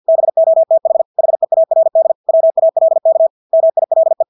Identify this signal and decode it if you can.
Morse code, heard as a keyed tone, about 50 wpm, 650 Hz. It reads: BOTH HEARD WALK MILE
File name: Engelske ord 50 wpm